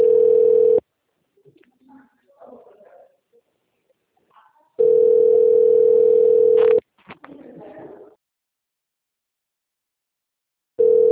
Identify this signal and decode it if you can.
Ringback tone (North America), 440 + 480 Hz.
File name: out- - - - - .16020